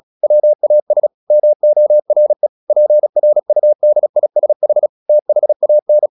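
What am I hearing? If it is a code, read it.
Morse code, heard as a keyed tone, about 36 wpm, 600 Hz. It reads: WAS MORE PRUDISH THAN
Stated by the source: Morse practice at 36 wpm, text of Ken Follett's novel Never, from CW med Gnister.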